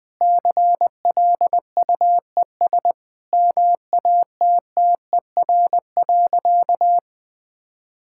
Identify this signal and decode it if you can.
Morse code, heard as a keyed tone, about 21 wpm, 700 Hz. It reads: CLUES MATTER.
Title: Sentences with 4-Word Phrases - 20wpm